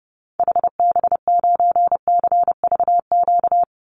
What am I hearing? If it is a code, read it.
Morse code, heard as a keyed tone, about 30 wpm, 700 Hz. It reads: HB9CVQ
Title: Call Signs Repeated 3x - 30wpm